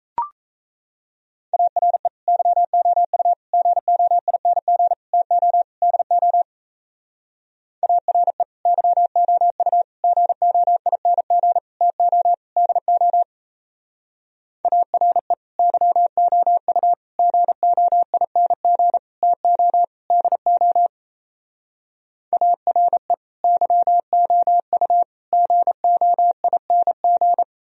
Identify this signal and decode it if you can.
Morse code, heard as a keyed tone, about 35 wpm, 700 Hz. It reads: ARE YOU GOING TO DO ARE YOU GOING TO DO ARE YOU GOING TO DO ARE YOU GOING